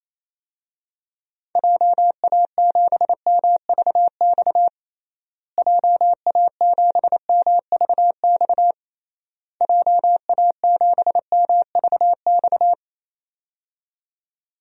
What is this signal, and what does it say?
Morse code, heard as a keyed tone, about 27 wpm, 700 Hz. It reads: JA7MVX JA7MVX JA7MVX
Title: DXCC-100 Call Signs - 6 Characters - Repeated 3x - 28wpm